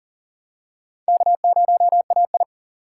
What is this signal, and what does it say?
Morse code, heard as a keyed tone, about 40 wpm, 700 Hz. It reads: K0AI
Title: USA Call Signs - 4x Character Spacing - 40wpm